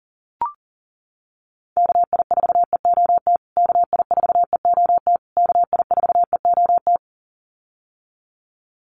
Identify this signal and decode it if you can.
Morse code, heard as a keyed tone, about 40 wpm, 700 Hz. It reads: KI4EOT KI4EOT KI4EOT
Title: USA Call Signs Repeated 3x 40wpm